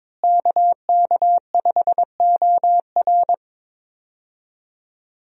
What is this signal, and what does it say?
Morse code, heard as a keyed tone, about 21 wpm, 700 Hz. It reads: KK5OR
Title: USA Call Signs - 5 Characters - Repeated 3x - 22wpm